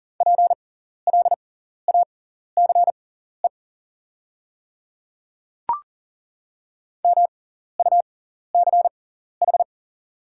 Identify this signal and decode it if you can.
Morse code, heard as a keyed tone, about 40 wpm, spaced as for 14 wpm, 690 Hz. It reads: PLACE MUCH
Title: 6x Character Spacing - Top 200 Words - 40wpm